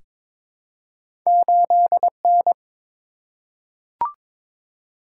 Morse code, heard as a keyed tone, about 22 wpm, 700 Hz. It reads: 8N
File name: practice / 22wpm / Two Letter-Number Combinations - 22wpm